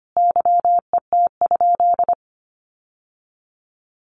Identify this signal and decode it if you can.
Morse code, heard as a keyed tone, about 25 wpm, 700 Hz. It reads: YET?
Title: Sentences from Top 1500 Words 25wpm